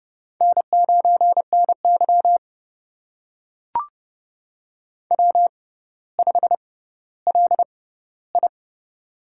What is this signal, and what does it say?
Morse code, heard as a keyed tone, about 28 wpm, spaced as for 15 wpm, 700 Hz. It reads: N9NY W5LI
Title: USA Call Signs - 6x Character Spacing - 30wpm